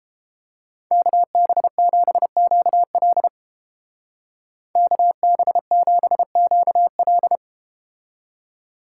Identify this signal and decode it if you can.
Morse code, heard as a keyed tone, about 31 wpm, 700 Hz. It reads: KB7QL KB7QL